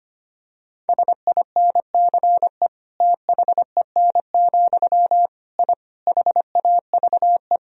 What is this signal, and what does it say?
Morse code, heard as a keyed tone, about 25 wpm, 700 Hz. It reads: SINCE THEN, I HAVE
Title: Sentences from Top 1000 Words - Review Part 1 - 25wpm